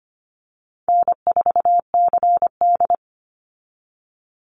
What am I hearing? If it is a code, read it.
Morse code, heard as a keyed tone, about 25 wpm, 700 Hz. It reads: N4CD
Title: Easy Call Signs Repeated 2x - 25wpm